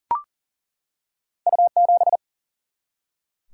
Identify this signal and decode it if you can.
Morse code, heard as a keyed tone, about 42 wpm, 700 Hz. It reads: U7
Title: Single-Letter Single-Number 40wpm